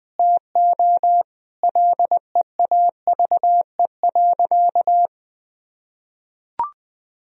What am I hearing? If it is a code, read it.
Morse code, heard as a keyed tone, about 20 wpm, 700 Hz. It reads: TO LEAVE.
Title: Sentences from Top 1000 Words - Review Part 4 - 20wpm